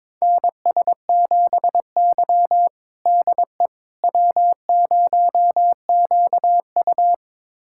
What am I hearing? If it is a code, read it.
Morse code, heard as a keyed tone, about 21 wpm, 700 Hz. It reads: NS7Y DE W0QU
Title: USA Callsign DE Callsign - 4 Characters - 22wpm